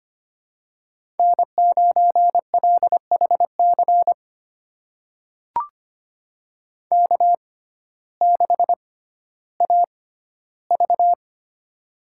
Morse code, heard as a keyed tone, about 25 wpm, spaced as for 14 wpm, 700 Hz. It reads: N9LHC K6AV
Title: USA Call Signs - 6x Character Spacing - 25wpm